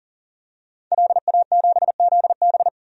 Morse code, heard as a keyed tone, about 40 wpm, 700 Hz. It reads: LA7ZB
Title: DXCC-100 Call Signs - 5 Characters - 40wpm